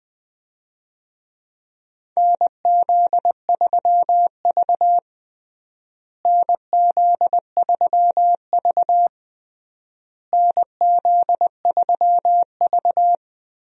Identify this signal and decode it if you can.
Morse code, heard as a keyed tone, about 20 wpm, 700 Hz. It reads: NZ3V NZ3V NZ3V
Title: USA Call Signs - 4 Characters - Repeated 3x - 20wpm